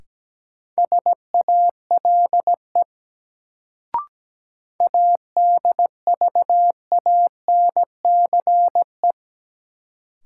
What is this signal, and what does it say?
Morse code, heard as a keyed tone, about 17 wpm, 700 Hz. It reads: SALE ADVANCE
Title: Sentences from Top 1400 Words - New Word Drill Alt - 17wpm